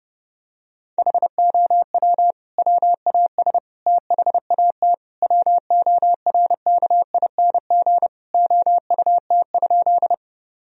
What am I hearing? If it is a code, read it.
Morse code, heard as a keyed tone, about 30 wpm, 700 Hz. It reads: HOW WAS THAT WORKING OUT?
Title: Sentences from Top 1000 Words - Review Part 4 - 30wpm